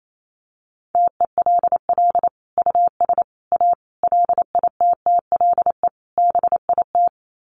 Morse code, heard as a keyed tone, about 28 wpm, 700 Hz. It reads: TELL US A LITTLE BIT
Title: Sets of 5 Words Encore2 28wpm